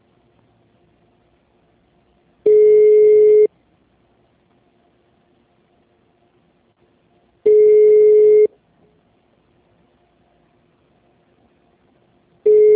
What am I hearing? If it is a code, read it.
Ringback tone (Europe, 425 Hz).
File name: out- -2027- - - .1786